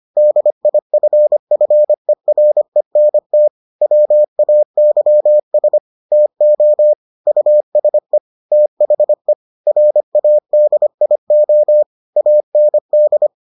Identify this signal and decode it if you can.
Morse code, heard as a keyed tone, about 25 wpm, 600 Hz. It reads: DIFFERENT WAYS TO USE THE RADIO AND